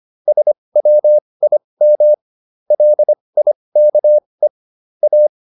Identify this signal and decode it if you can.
Morse code, heard as a keyed tone, about 25 wpm, spaced as for 20 wpm, 600 Hz. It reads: SWIM LIKE A